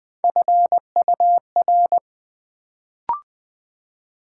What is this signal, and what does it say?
Morse code, heard as a keyed tone, about 20 wpm, 700 Hz. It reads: FUR
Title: Build A Word - 1 Letter At A Time - 3-Letter Words - 20wpm